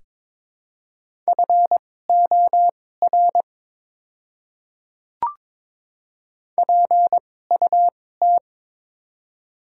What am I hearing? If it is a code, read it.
Morse code, heard as a keyed tone, about 23 wpm, spaced as for 16 wpm, 700 Hz. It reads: FOR PUT
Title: 2x Character Spacing - Top 200 Words - 22wpm